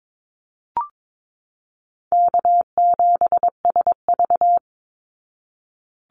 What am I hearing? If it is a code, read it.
Morse code, heard as a keyed tone, about 22 wpm, 700 Hz. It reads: K7SV
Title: Easy Call Signs Repeated 3x - 22wpm